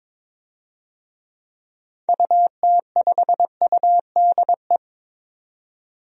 Morse code, heard as a keyed tone, about 22 wpm, 700 Hz. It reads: UT5UDE
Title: DXCC-100 Call Signs - 6 Characters - 22wpm